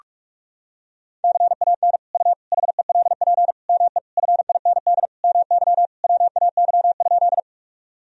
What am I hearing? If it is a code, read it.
Morse code, heard as a keyed tone, about 46 wpm, 700 Hz. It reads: CAN U HELP ME FIND MY WAY?
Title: Sentences from Top 200 Words and Abbreviations 45wpm